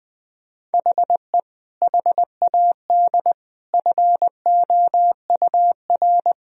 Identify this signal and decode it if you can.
Morse code, heard as a keyed tone, about 21 wpm, 700 Hz. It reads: HE HAD FOUR
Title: Sentences from Top 1000 Words - Review Part 1 - 20wpm